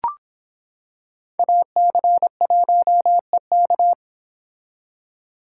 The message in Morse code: AC1EK

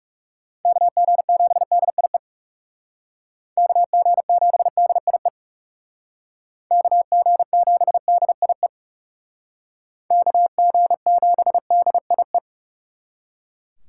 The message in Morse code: KG7DIE KG7DIE KG7DIE KG7DIE

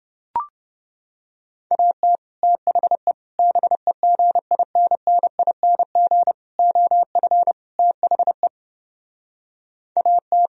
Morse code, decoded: AT THE BEGINNING OF THE AT

